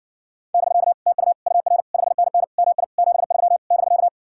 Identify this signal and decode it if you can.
Morse code, heard as a keyed tone, about 60 wpm, 700 Hz. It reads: <BK> TU UR 5NN MI 73 <BK>